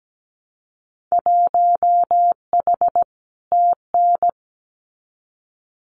Morse code, heard as a keyed tone, about 17 wpm, 700 Hz. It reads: 1H TN
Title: Winter Field Day Exchanges 17wpm